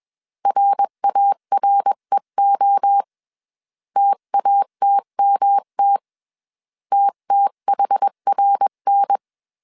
Morse code, heard as a keyed tone, about 21 wpm, 800 Hz. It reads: RAREO TATMT TTHRN
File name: 20wpm